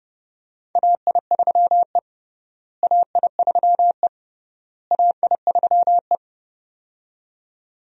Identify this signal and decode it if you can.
Morse code, heard as a keyed tone, about 30 wpm, 700 Hz. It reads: AI3E AI3E AI3E
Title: USA Call Signs - 4 Characters - Repeated 3x - 30wpm